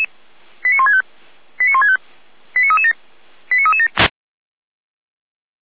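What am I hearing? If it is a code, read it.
Five-tone selective call (ZVEI), heard as quick runs of five tones, tones of about 80 ms. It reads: 89165 89165 89337 89337
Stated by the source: ZVEI five-tone call received by SDR, from feuerwehr-orlamuende fire service audio archive.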